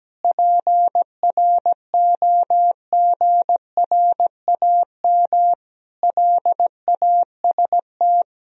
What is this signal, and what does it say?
Morse code, heard as a keyed tone, about 17 wpm, 700 Hz. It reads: PROGRAM LAST
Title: Sentences from Top 1000 Words - Review Part 4 - 17wpm